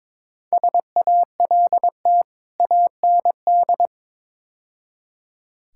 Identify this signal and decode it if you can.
Morse code, heard as a keyed tone, about 22 wpm, 700 Hz. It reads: SALT AND
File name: practice / 22wpm / Build A Phrase - 5-Word Phrases - 22wpm